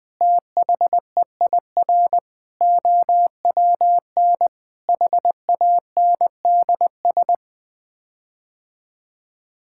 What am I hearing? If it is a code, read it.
Morse code, heard as a keyed tone, about 20 wpm, 700 Hz. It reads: THEIR OWN HANDS
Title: Sets of 8 Words - 20wpm